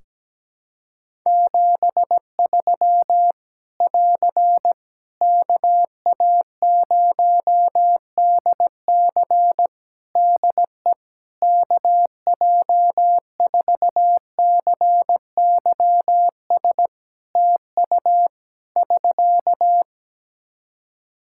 Morse code, heard as a keyed tone, about 17 wpm, 700 Hz. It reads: 73 <AR> KA0DC DE KJ4CYS TU <SK>
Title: K1USN SST QSO Practice 17wpm